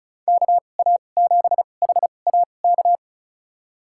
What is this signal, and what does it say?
Morse code, heard as a keyed tone, about 34 wpm, spaced as for 25 wpm, 700 Hz. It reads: KA7HAK
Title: USA Call Signs - 6 Characters - 2x Character Spacing - 35wpm